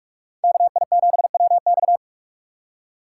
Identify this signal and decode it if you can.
Morse code, heard as a keyed tone, about 43 wpm, 700 Hz. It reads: KI7WX